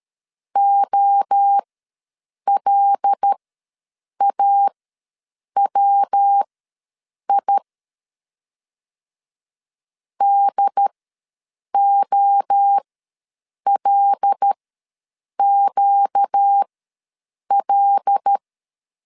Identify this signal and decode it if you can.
Morse code, heard as a keyed tone, about 13 wpm, spaced as for 7 wpm, 800 Hz. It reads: OLAWI DOLQL
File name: LEZIONI / 5 LBVC / 5 L ws 12wpm 2958